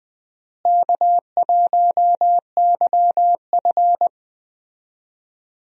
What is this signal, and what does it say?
Morse code, heard as a keyed tone, about 20 wpm, 700 Hz. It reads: K1YF
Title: USA Call Signs - 4 Characters - Repeated 3x - 20wpm